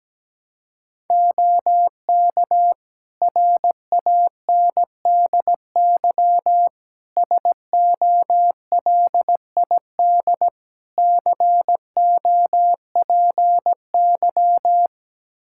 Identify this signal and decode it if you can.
Morse code, heard as a keyed tone, about 17 wpm, 700 Hz. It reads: OK RANDY SOLID COPY
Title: Practice QSO - Exchange 3 - More Info - 17wpm